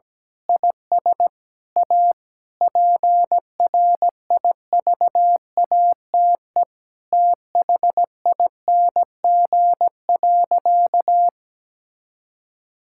Morse code, heard as a keyed tone, about 17 wpm, 700 Hz. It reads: IS A PRIVATE THING.
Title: Sentences from Top 1600 Words 17wpm